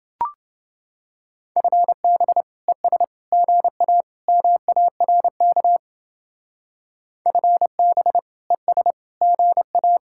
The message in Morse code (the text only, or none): FB ES GA MARK FB ES GA